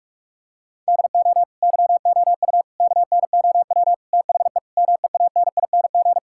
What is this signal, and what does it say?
Morse code, heard as a keyed tone, about 46 wpm, 700 Hz. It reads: DO YOU KNOW THE MEANING
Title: Sentences from Top 1000 Words - Review Part 6 - 45wpm